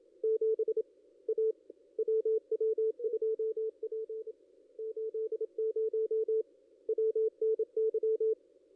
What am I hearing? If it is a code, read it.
Morse code, heard as a keyed tone, about 29 wpm, 440 Hz, with radio noise. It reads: 7 A WW2P 80 WNY
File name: contests / ss-cw-k5tr-op-n5rz -1500